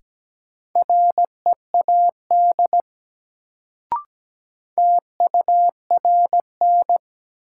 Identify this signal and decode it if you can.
Morse code, heard as a keyed tone, about 17 wpm, 700 Hz. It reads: READ TURN
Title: Sentences from Top 200 Words - New Word Drill - 17wpm